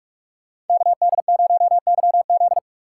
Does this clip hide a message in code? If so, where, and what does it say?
Yes: Morse code from 0.7 to 2.6 s, heard as a keyed tone, about 46 wpm, 700 Hz. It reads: KD0YZ